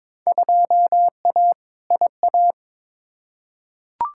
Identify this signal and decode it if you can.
Morse code, heard as a keyed tone, about 22 wpm, 700 Hz. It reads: 2A IA
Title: Field Day Exchanges Encore - 22wpm